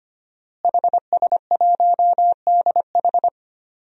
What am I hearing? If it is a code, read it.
Morse code, heard as a keyed tone, about 25 wpm, 700 Hz. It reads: HS1DH
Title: DXCC-100 Call Signs - 5 Characters - Repeated 2x - 25wpm